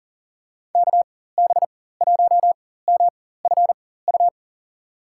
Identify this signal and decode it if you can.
Morse code, heard as a keyed tone, about 40 wpm, spaced as for 19 wpm, 700 Hz. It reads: KB1MFU
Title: USA Call Signs - 6 Characters - 4x Character Spacing - 40wpm